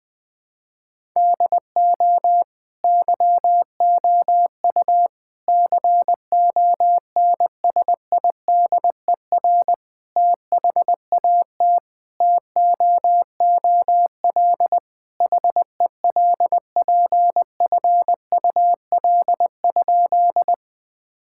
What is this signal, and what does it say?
Morse code, heard as a keyed tone, about 21 wpm, 700 Hz. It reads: DO YOU CONSIDER THAT TOOL HELPFUL?
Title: Sentences from Top 1000 Words - Review Part 6 - 20wpm